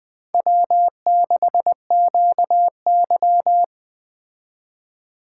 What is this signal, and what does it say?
Morse code, heard as a keyed tone, about 20 wpm, 700 Hz. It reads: W6QY
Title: Speed Racing - USA 1x2 Call Signs 17wpm